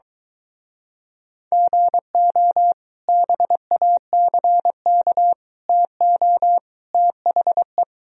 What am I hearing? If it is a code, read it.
Morse code, heard as a keyed tone, about 23 wpm, 700 Hz. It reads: GO BACK TO THE